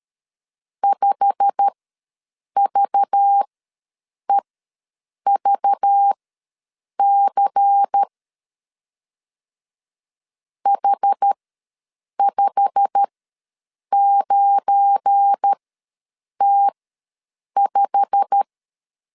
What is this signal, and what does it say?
Morse code, heard as a keyed tone, about 13 wpm, spaced as for 7 wpm, 800 Hz. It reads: 5VEVC H59T5
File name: LEZIONI / 8 YJ45 / 8 YJ45 ws 12wpm 3146